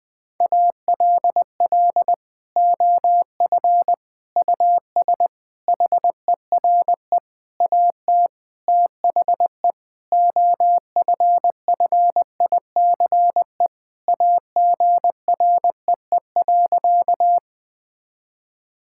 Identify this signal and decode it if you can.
Morse code, heard as a keyed tone, about 21 wpm, 700 Hz. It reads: ALL OF US HERE AT THE OFFICE AGREE.
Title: Sentences with 5-Word Phrases - Spoken Beginning - 20wpm